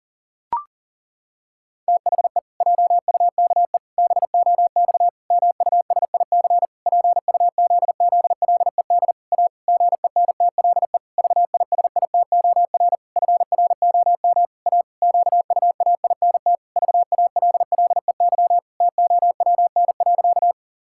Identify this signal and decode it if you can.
Morse code, heard as a keyed tone, about 40 wpm, 700 Hz. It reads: THE JUKE BOX MUSIC PUZZLED A GENTLE VISITOR FROM A QUAINT VALLEY TOWN.
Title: Pangram Challenge v2 - 40wpm